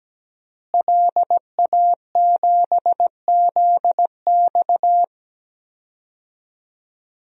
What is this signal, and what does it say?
Morse code, heard as a keyed tone, about 17 wpm, 700 Hz. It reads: LA7ZX
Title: DXCC-100 Call Signs - 5 Characters - Repeated 2x - 17wpm